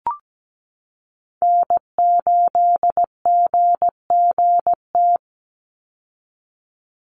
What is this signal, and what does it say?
Morse code, heard as a keyed tone, about 17 wpm, 700 Hz. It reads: N8GGT
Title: USA Call Signs 17wpm